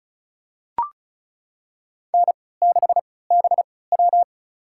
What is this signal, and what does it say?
Morse code, heard as a keyed tone, about 34 wpm, spaced as for 20 wpm, 700 Hz. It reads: N6BW
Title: USA Call Signs - 4 Characters - 3x Character Spacing - 35wpm